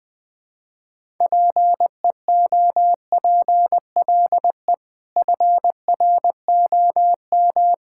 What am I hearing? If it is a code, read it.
Morse code, heard as a keyed tone, about 20 wpm, 700 Hz. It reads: PEOPLE FROM
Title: Sentences with 6-Word Phrases - Spoken Beginning - 20wpm